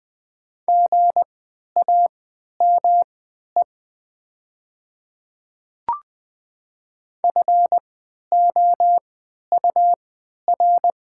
Morse code, heard as a keyed tone, about 20 wpm, spaced as for 11 wpm, 700 Hz. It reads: GAME FOUR
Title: 3x Character Spacing - Top 500 Words - 20wpm